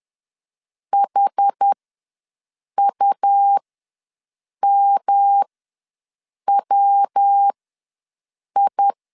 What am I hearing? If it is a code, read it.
Morse code, heard as a keyed tone, about 10 wpm, spaced as for 6 wpm, 800 Hz. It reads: HUMWI
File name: LEZIONI / 5 LBVC / 5 LB ws 10wpm 3616